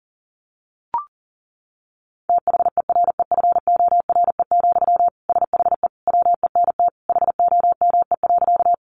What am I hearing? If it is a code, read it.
Morse code, heard as a keyed tone, about 40 wpm, 700 Hz. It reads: THEREFORE, SHE WENT HOME.
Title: Sentences from Top 800 Words 40wpm